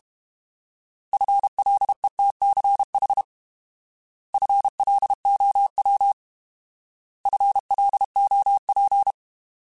Morse code, heard as a keyed tone, about 32 wpm, 800 Hz. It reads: FLETCH FLOW FLOP
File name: Alphabetical / Alphabetical mp3s / Fs @30